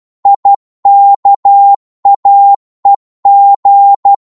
Morse code, heard as a keyed tone, about 12 wpm, 800 Hz. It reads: IKAEG
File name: Morse / ltr-12wpm